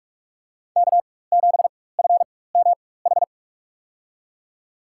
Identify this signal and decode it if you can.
Morse code, heard as a keyed tone, about 46 wpm, spaced as for 21 wpm, 700 Hz. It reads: K7FMH
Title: USA Call Signs - 5 Characters - 4x Character Spacing - 45wpm